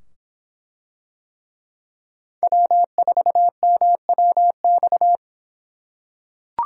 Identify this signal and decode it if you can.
Morse code, heard as a keyed tone, about 24 wpm, 700 Hz. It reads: W4MWX